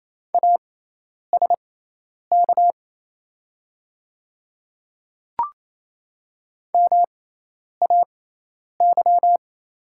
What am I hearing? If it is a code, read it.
Morse code, heard as a keyed tone, about 27 wpm, spaced as for 10 wpm, 700 Hz. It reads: ASK MAY
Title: 6x Character Spacing - Top 200 Words - 28wpm